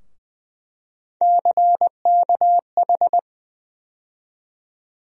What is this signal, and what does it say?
Morse code, heard as a keyed tone, about 21 wpm, 700 Hz. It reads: CKH
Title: USA Call Sign Postfixes - 3 Letters 20wpm